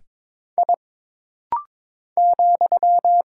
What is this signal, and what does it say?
Morse code, heard as a keyed tone, about 22 wpm, 700 Hz. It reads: I ,
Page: This rapid-fire timing is geared to push your ICR (Instant Character Recognition) skills.